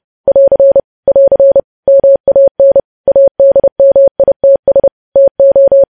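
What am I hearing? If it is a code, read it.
Morse code, heard as a keyed tone, about 30 wpm, 550 Hz. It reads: <AR> <AR> MAN ADMITS TO
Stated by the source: News headlines sent as Morse practice at 30.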